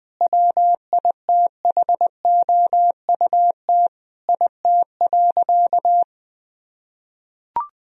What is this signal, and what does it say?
Morse code, heard as a keyed tone, about 20 wpm, 700 Hz. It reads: WITHOUT IT.
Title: Sentences from Top 1000 Words - Review Part 3 - 20wpm